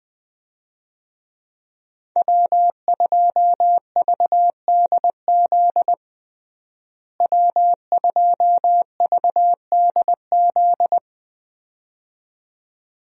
Morse code, heard as a keyed tone, about 20 wpm, 700 Hz. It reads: W2VDZ W2VDZ
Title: USA Call Signs - 5 Characters - Repeated 2x - 20wpm